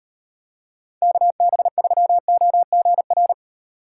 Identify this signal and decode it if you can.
Morse code, heard as a keyed tone, about 38 wpm, 700 Hz. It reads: KB3OGR